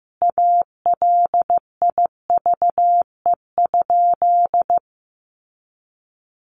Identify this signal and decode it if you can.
Morse code, heard as a keyed tone, about 15 wpm, 700 Hz. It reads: ALIVE?
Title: A Sound Of Thunder 15wpm